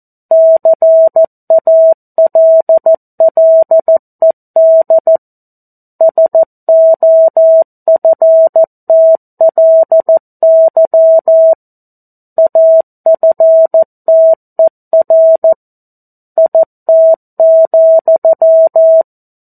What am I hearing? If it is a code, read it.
Morse code, heard as a keyed tone, about 14 wpm, 650 Hz. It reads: CALLED SOFTLY AFTER IT,
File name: 02-Chapter-02 14 WPM